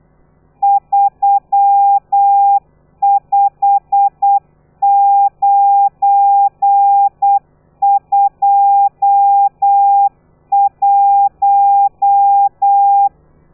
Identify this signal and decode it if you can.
Morse code, heard as a keyed tone, about 8 wpm, 800 Hz. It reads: 35921
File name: telegrafia / i2tfj / Lez12 35